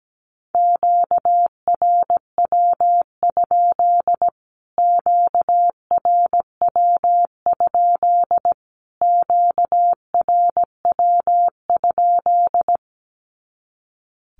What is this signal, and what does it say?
Morse code, heard as a keyed tone, about 17 wpm, 700 Hz. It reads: QRW? QRW? QRW?
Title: Q-Codes Repeated 3x 17wpm